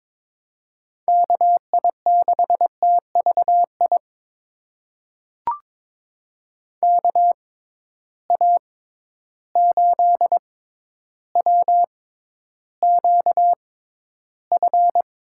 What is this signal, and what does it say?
Morse code, heard as a keyed tone, about 22 wpm, spaced as for 11 wpm, 700 Hz. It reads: KI6TVI KA8WQF
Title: USA Call Signs - 6 Characters - 6x Character Spacing - 22wpm